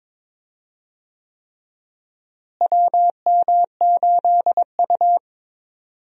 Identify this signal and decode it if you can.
Morse code, heard as a keyed tone, about 22 wpm, 700 Hz. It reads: WM8U
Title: USA Call Signs - 4 Characters - Repeated 3x - 22wpm